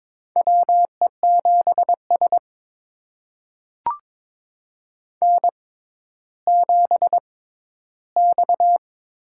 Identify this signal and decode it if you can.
Morse code, heard as a keyed tone, about 22 wpm, spaced as for 12 wpm, 700 Hz. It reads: WE7S N7X
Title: USA Call Signs - 4 Characters - 6x Character Spacing - 22wpm